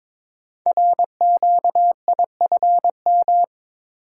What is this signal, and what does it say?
Morse code, heard as a keyed tone, about 22 wpm, 700 Hz. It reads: RQIFM
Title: Code Groups - 5 Letters - 22wpm